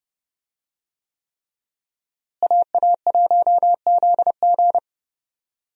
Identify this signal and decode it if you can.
Morse code, heard as a keyed tone, about 30 wpm, 700 Hz. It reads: AA1ZG
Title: USA Call Signs - 5 Characters - Repeated 3x - 30wpm